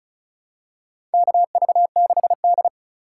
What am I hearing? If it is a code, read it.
Morse code, heard as a keyed tone, about 36 wpm, 700 Hz. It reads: KV6D
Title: USA Call Signs - 2x Character Spacing - 35wpm